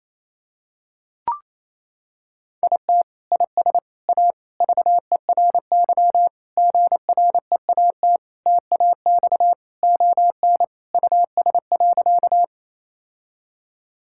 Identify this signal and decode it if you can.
Morse code, heard as a keyed tone, about 28 wpm, 700 Hz. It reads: IT IS A VERY GREAT TAX ON US.
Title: Sentences from Top 1000 Words - Review Part 5 - 28wpm